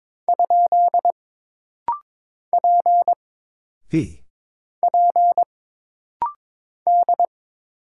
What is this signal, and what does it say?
Morse code, heard as a keyed tone, about 22 wpm, 700 Hz. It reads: ? P P D
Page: This rapid-fire timing is geared to push your ICR (Instant Character Recognition) skills.